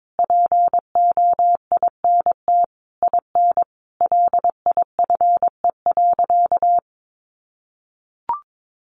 Morse code, heard as a keyed tone, about 22 wpm, 700 Hz. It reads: POINT IN LIFE.
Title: Sentences from Top 1200 Words 22wpm